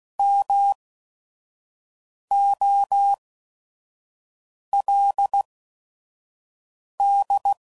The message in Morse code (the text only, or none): MOLD